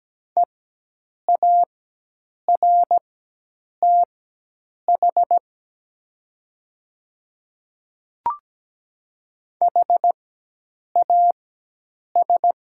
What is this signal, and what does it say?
Morse code, heard as a keyed tone, about 17 wpm, spaced as for 8 wpm, 700 Hz. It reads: EARTH HAS
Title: 4x Character Spacing - Top 300 Words - 17wpm